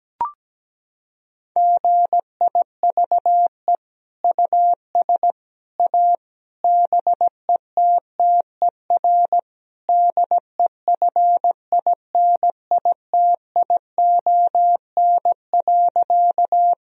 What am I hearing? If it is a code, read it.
Morse code, heard as a keyed tone, about 17 wpm, 700 Hz. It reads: GIVE US A BETTER DEFINITION.
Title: Sentences from Top 1000 Words - Review Part 4 - 17wpm